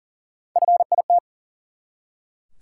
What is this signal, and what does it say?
Morse code, heard as a keyed tone, about 40 wpm, 700 Hz. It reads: FIT
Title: Three Letter Words - 40wpm